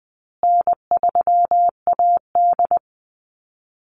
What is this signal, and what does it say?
Morse code, heard as a keyed tone, about 20 wpm, 700 Hz. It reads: N3AD
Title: Call Signs Repeated 2x - 20wpm